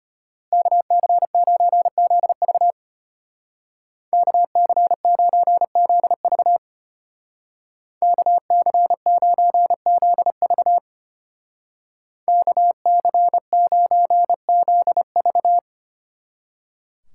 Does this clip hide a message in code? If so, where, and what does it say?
Yes: Morse code from 0.5 to 15.6 s, heard as a keyed tone, about 32 wpm, 700 Hz. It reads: KC9ZV KC9ZV KC9ZV KC9ZV